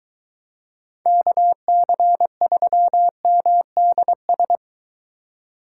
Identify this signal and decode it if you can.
Morse code, heard as a keyed tone, about 23 wpm, 700 Hz. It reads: KC3MDS